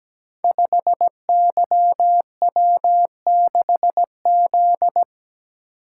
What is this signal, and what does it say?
Morse code, heard as a keyed tone, about 17 wpm, 700 Hz. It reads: 5YW6Z
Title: Code Groups - 5 Letter:Numbers - 17wpm